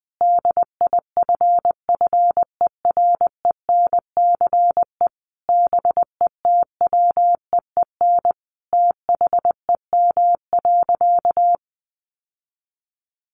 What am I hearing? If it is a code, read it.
Morse code, heard as a keyed tone, about 20 wpm, 700 Hz. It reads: DIFFERENCE BETWEEN THEM.
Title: Sentences from Top 800 Words 20wpm